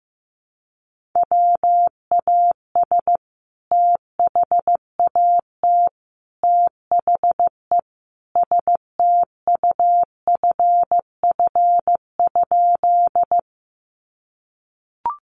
Morse code, heard as a keyed tone, about 14 wpm, 700 Hz. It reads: WAS THAT THE STUFF?
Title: Sentences from Top 1400 Words 15wpm